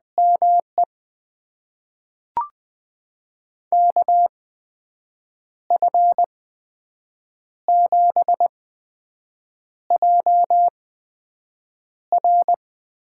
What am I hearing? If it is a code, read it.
Morse code, heard as a keyed tone, about 20 wpm, spaced as for 6 wpm, 700 Hz. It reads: ME KF7JR
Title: USA Call Signs - 8x Character Spacing - 20wpm